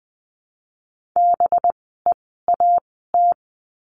Morse code, heard as a keyed tone, about 20 wpm, spaced as for 14 wpm, 700 Hz. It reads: BEAT
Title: 185 Lesson 30f - Letter b - Words 20wpm